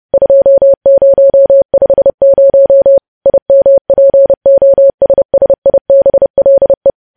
Morse code, heard as a keyed tone, about 30 wpm, 550 Hz. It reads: 2050 IMPOSSIBLE
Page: news headlines in morse code at 30 wpm